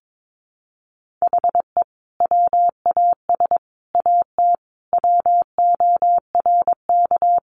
Morse code, heard as a keyed tone, about 22 wpm, 700 Hz. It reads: HE WAS AT WORK
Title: Sentences from Top 200 Words 22wpm